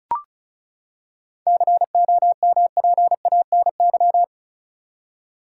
Morse code, heard as a keyed tone, about 34 wpm, 700 Hz. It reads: COMPANY
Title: Seven Letter Words - 35wpm